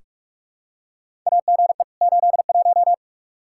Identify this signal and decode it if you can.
Morse code, heard as a keyed tone, about 43 wpm, 700 Hz. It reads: AGE 81
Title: Practice QSO - Exchange 3 - More Info - 45wpm